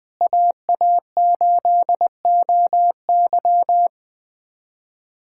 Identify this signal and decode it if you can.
Morse code, heard as a keyed tone, about 20 wpm, 700 Hz. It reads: AA8OY
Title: K1USN SST QSO Practice - 20wpm